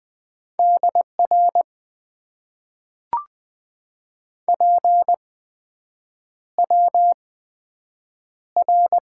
Morse code, heard as a keyed tone, about 20 wpm, spaced as for 7 wpm, 700 Hz. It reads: DR PWR
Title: Top 90 QSO Elements - 8x Character Spacing - 20wpm